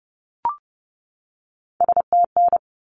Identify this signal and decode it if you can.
Morse code, heard as a keyed tone, about 30 wpm, 700 Hz. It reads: STN
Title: Abbreviations 30wpm